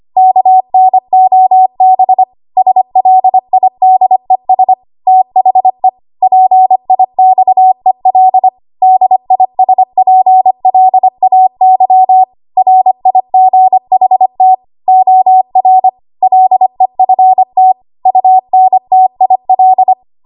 Morse code, heard as a keyed tone, about 25 wpm, 750 Hz. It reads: KNOB SLIDES THE PIXEL DISPLAY RIGHT OR LEFT UNTIL
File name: Archive / 25 WPM / 25WPM